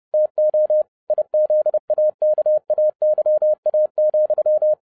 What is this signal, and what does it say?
Morse code, heard as a keyed tone, about 30 wpm, 610 Hz. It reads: TO IZAKAYA,
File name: 2011-07-mp3 / n60 BACK FROM A BIG ACCIDENT 30wpm